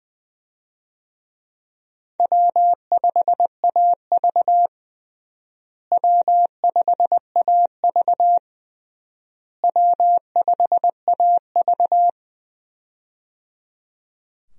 Morse code, heard as a keyed tone, about 20 wpm, 700 Hz. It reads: W5AV W5AV W5AV
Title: USA Call Signs - 4 Characters - Repeated 3x - 20wpm